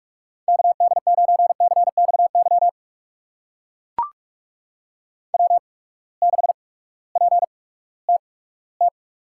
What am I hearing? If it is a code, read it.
Morse code, heard as a keyed tone, about 44 wpm, spaced as for 20 wpm, 700 Hz. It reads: KD9CXY W6PTT
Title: USA Call Signs - 8x Character Spacing - 45wpm